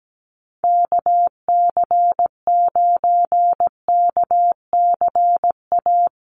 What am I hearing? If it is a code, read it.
Morse code, heard as a keyed tone, about 17 wpm, 700 Hz. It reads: KC9KCA
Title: USA Call Signs Repeated 3x 17wpm